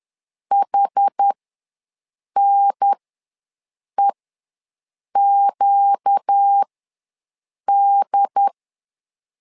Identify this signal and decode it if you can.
Morse code, heard as a keyed tone, about 10 wpm, spaced as for 6 wpm, 800 Hz. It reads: HNEQD
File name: LEZIONI / 4 Q1 ws 10wpm 2538